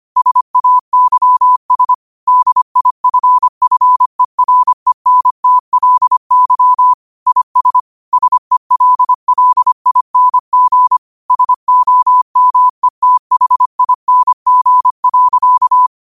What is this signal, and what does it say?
Morse code, heard as a keyed tone, about 25 wpm, 1000 Hz. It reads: SAYS DIFFERENTLY IS SELLING SOMETHING.